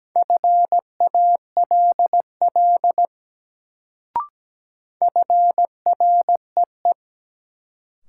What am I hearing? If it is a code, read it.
Morse code, heard as a keyed tone, about 17 wpm, 700 Hz. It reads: FALL FREE
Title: Sentences from Top 500 Words - New Word Drill - 17wpm